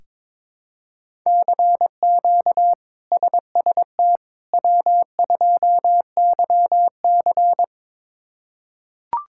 Morse code, heard as a keyed tone, about 22 wpm, 700 Hz. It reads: CQ SST W2YC
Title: K1USN SST QSO Practice - 22wpm